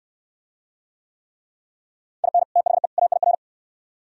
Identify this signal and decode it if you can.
Morse code, heard as a keyed tone, about 60 wpm, 700 Hz. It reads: IN THE AIR